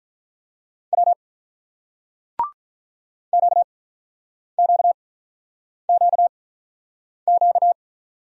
Morse code, heard as a keyed tone, about 48 wpm, spaced as for 9 wpm, 700 Hz. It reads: W QQQQ